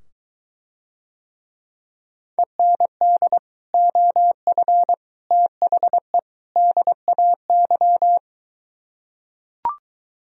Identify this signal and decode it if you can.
Morse code, heard as a keyed tone, about 23 wpm, 700 Hz. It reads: END OF THE DAY